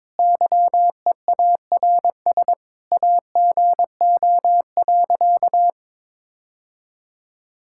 Morse code, heard as a keyed tone, about 22 wpm, 700 Hz. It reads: YEARS AGO.
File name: practice / 22wpm / Sentences with 2-Word Phrases - 22wpm